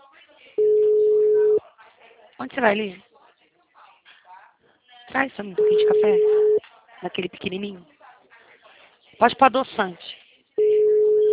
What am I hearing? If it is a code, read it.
Ringback tone (Europe, 425 Hz), with speech.